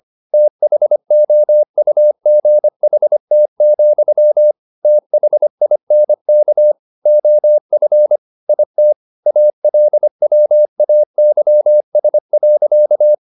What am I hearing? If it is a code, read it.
Morse code, heard as a keyed tone, about 25 wpm, 600 Hz. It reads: THOUGHT, THINK OF IT ALWAYS.
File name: the old man and the sea part 1 25wpm